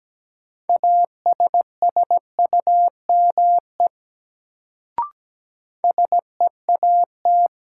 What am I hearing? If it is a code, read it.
Morse code, heard as a keyed tone, about 17 wpm, 700 Hz. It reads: ASSUME SEAT
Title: Sentences from Top 1500 Words - New Word Drill Alt - 17wpm